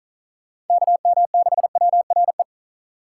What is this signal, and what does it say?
Morse code, heard as a keyed tone, about 46 wpm, 700 Hz. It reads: KM6WRE